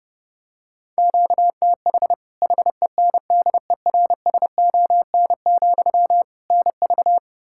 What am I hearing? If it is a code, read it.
Morse code, heard as a keyed tone, about 30 wpm, 700 Hz. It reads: QTH HENDERSON, NV
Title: Practice QSO - Exchange 1 - Beginning - 30wpm